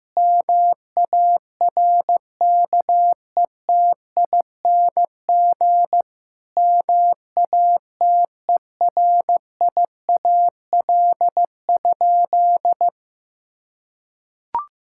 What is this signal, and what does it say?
Morse code, heard as a keyed tone, about 16 wpm, 700 Hz. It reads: MARKETING MATERIAL?
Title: Sentences from Top 1000 Words - Review Part 5 - 15wpm